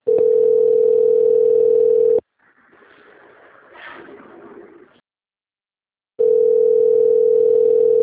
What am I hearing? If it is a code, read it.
Ringback tone (North America), 440 + 480 Hz.